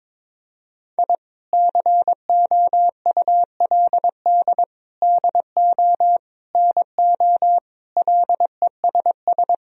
Morse code, heard as a keyed tone, about 22 wpm, 700 Hz. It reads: I COULD DO NO LESS
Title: Sentences from Top 1000 Words - Review Part 3 - 22wpm